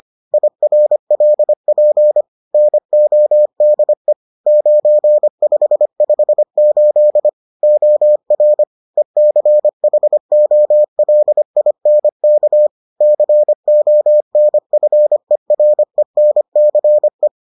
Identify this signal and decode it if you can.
Morse code, heard as a keyed tone, about 25 wpm, 600 Hz. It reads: IRLP NODE 9558 OR ECHOLINK CONFERENCE